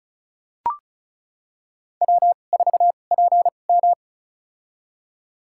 Morse code, heard as a keyed tone, about 34 wpm, spaced as for 25 wpm, 700 Hz. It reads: W4PM
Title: USA Call Signs - 4 Characters - 2x Character Spacing - 35wpm